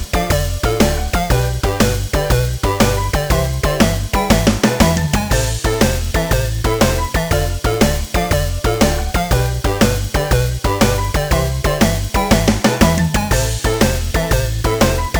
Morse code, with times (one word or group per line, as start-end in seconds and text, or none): none